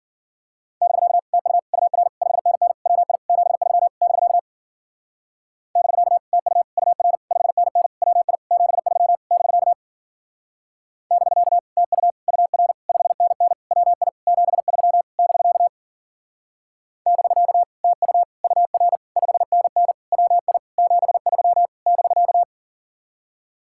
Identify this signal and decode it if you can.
Morse code, heard as a keyed tone, about 50 wpm, 700 Hz. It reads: <BK> TU UR 5NN WI 73 <BK> <BK> TU UR 5NN WI 73 <BK> <BK> TU UR 5NN WI 73 <BK> <BK> TU UR 5NN WI 73 <BK>